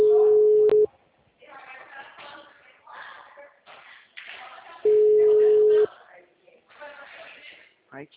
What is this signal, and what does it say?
Ringback tone (Europe, 425 Hz), with speech.